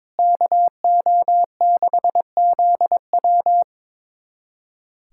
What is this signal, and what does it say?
Morse code, heard as a keyed tone, about 22 wpm, 700 Hz. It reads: KO6ZW
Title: USA Callsign DE Callsign - 5 Characters - 22wpm